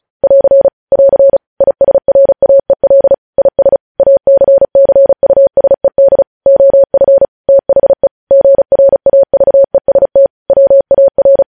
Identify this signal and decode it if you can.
Morse code, heard as a keyed tone, about 36 wpm, 550 Hz. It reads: <AR> <AR> ISRAEL IS ACCUSED OF THE GRAVEST WAR
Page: news headlines in morse code at 35 wpm